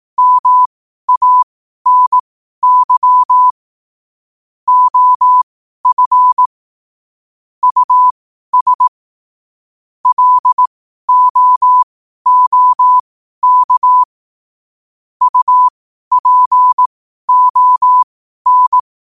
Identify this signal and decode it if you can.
Morse code, heard as a keyed tone, about 18 wpm, spaced as for 12 wpm, 1000 Hz. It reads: MANY OF US LOOK UPON
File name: qotdfiles / 12wpm